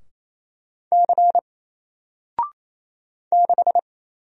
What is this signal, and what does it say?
Morse code, heard as a keyed tone, about 28 wpm, 700 Hz. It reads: C 6
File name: Single-Letter-Number-28wpm